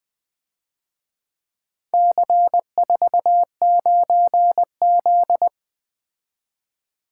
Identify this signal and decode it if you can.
Morse code, heard as a keyed tone, about 23 wpm, 700 Hz. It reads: C49Z
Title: Speed-Racing - Four Letter-Number Combinations - 20wpm